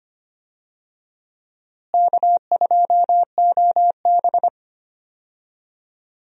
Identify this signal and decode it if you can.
Morse code, heard as a keyed tone, about 25 wpm, 700 Hz. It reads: K2OB
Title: USA Call Signs - 4 Characters - Repeated 2x - 25wpm